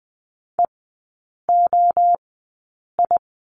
Morse code, heard as a keyed tone, about 20 wpm, spaced as for 8 wpm, 700 Hz. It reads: EOI
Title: 053 Lesson 10e - Letter r - Three Characters 20wpm